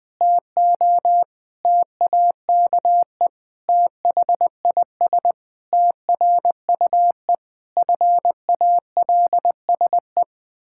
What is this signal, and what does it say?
Morse code, heard as a keyed tone, about 21 wpm, 700 Hz. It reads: TO TAKE THIS TRUE FALSE
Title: Pangram Challenge v2 - 20wpm